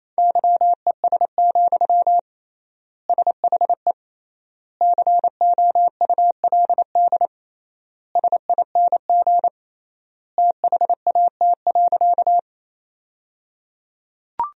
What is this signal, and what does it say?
Morse code, heard as a keyed tone, about 28 wpm, 700 Hz. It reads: YES, SHE COULD SING THAT.
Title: 3x Word-Spacing - Sentences from Top 500 Words - 28wpm